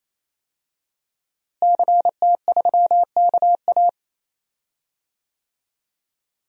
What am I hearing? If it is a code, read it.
Morse code, heard as a keyed tone, about 28 wpm, 700 Hz. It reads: CT3KA
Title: DXCC-100 Call Signs - 5 Characters - Repeated 3x - 28wpm